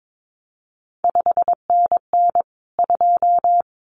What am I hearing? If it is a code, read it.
Morse code, heard as a keyed tone, about 22 wpm, 700 Hz. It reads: 5NN 2
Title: CQ WW DX Contest Exchange Encore - 22wpm